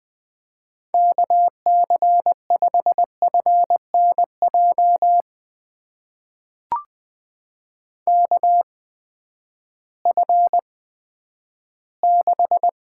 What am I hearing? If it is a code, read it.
Morse code, heard as a keyed tone, about 20 wpm, spaced as for 11 wpm, 700 Hz. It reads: KC5FNJ KF6
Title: USA Call Signs - 6 Characters - 8x Character Spacing - 20wpm